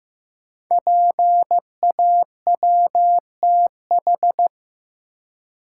Morse code, heard as a keyed tone, about 15 wpm, 700 Hz. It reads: PAWTH
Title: Code Groups - 5 Letters - 15wpm